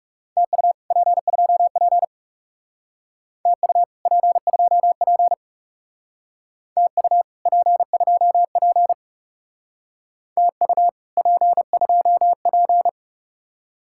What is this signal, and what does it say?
Morse code, heard as a keyed tone, about 38 wpm, 700 Hz. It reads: TU P2P TU P2P TU P2P TU P2P